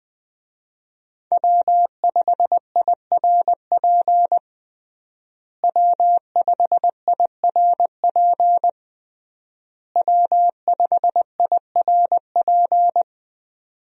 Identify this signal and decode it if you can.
Morse code, heard as a keyed tone, about 20 wpm, 700 Hz. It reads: W5IRP W5IRP W5IRP
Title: USA Call Signs - 5 Characters - Repeated 3x - 20wpm